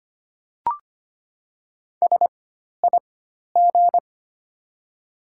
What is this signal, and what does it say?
Morse code, heard as a keyed tone, about 25 wpm, spaced as for 12 wpm, 700 Hz. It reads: SIG
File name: Top 90 QSO Elements - 4x Character Spacing - 25wpm